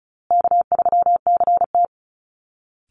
Morse code, heard as a keyed tone, about 34 wpm, 700 Hz. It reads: K3CT
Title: Call Signs Repeated 2x - 35wpm